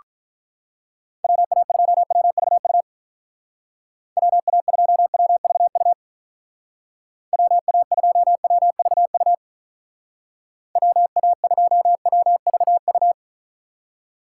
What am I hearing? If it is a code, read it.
Morse code, heard as a keyed tone, about 45 wpm, 700 Hz. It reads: WA2WVU WA2WVU WA2WVU WA2WVU